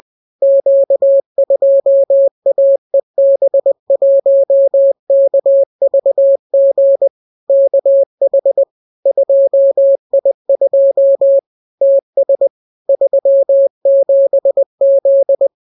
Morse code, heard as a keyed tone, about 20 wpm, 550 Hz. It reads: Q2AEB1KVG KH 2I2 TS 37Z